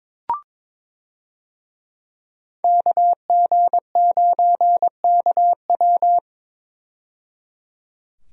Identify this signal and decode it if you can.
Morse code, heard as a keyed tone, about 22 wpm, 700 Hz. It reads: KG9KW
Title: USA Call Signs - 5 Characters - 22wpm